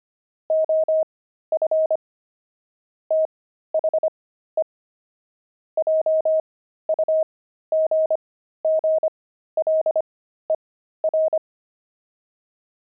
Morse code, heard as a keyed tone, about 25 wpm, spaced as for 13 wpm, 630 Hz. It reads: OF THE JUGGLER
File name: pangram-fs-13-cs-25-dexterity